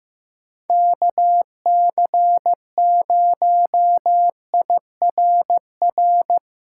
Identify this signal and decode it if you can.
Morse code, heard as a keyed tone, about 15 wpm, 700 Hz. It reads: KC0IRR
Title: USA Callsign DE Callsign 15wpm